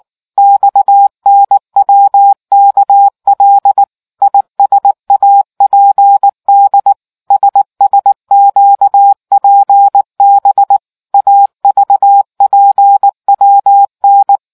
Morse code, heard as a keyed tone, about 19 wpm, 800 Hz. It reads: XNWKL ISAPD SSQPB AVPWN